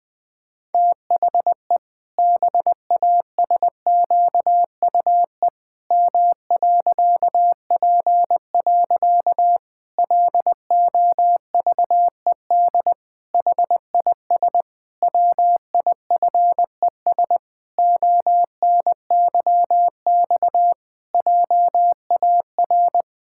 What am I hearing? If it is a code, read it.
Morse code, heard as a keyed tone, about 20 wpm, 700 Hz. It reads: THE BASQUE M.P. LOVED HIS WIFES ONYX JAR
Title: Pangram Challenge v2 - 20wpm